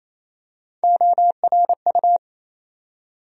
Morse code, heard as a keyed tone, about 28 wpm, 700 Hz. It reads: ORU
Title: USA Call Sign Postfixes - 3 Letters 28wpm